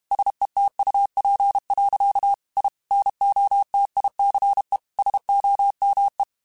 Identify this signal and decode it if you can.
Morse code, heard as a keyed tone, about 32 wpm, 800 Hz. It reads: SETUP. I NOTICE SOME
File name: QSOs / QSOs-mp3s / WD0BGP 1@30